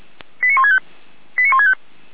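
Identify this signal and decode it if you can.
Five-tone selective call (ZVEI), heard as quick runs of five tones, tones of about 70 ms. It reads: 89165 89165
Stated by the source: ZVEI five-tone selective call received by SDR.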